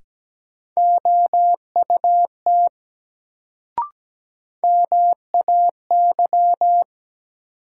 Morse code, heard as a keyed tone, about 17 wpm, 700 Hz. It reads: OUT MAY
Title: Sentences from Top 100 Words - New Word Drill - 17wpm